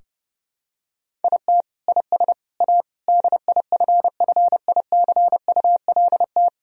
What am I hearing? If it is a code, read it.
Morse code, heard as a keyed tone, about 30 wpm, 700 Hz. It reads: IT IS A DIFFICULT
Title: Sentences from Top 1000 Words - Review Part 3 - 30wpm